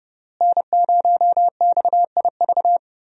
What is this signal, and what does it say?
Morse code, heard as a keyed tone, about 30 wpm, 700 Hz. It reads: N0XIV
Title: USA Call Signs - 5 Characters - Repeated 3x - 30wpm